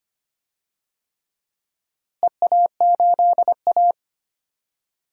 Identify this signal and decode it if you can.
Morse code, heard as a keyed tone, about 25 wpm, 700 Hz. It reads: EA8A
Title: DXCC-100 Call Signs - 4 Characters - Repeated 3x - 25wpm